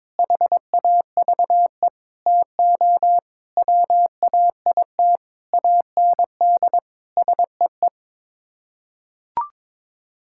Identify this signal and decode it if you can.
Morse code, heard as a keyed tone, about 22 wpm, 700 Hz. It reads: HAVE TO WAIT AND SEE
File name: practice / 22wpm / Sets of 7 Words - 22wpm